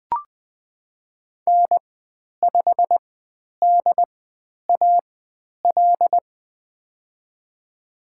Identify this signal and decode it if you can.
Morse code, heard as a keyed tone, about 20 wpm, spaced as for 10 wpm, 700 Hz. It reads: N5DAL
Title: 095 Lesson 16g - Letter c - Call Signs 20wpm